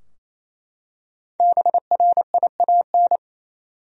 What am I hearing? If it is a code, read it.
Morse code, heard as a keyed tone, about 28 wpm, 700 Hz. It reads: BRIAN
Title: USA Common Names - Max 5 Letters - 28wpm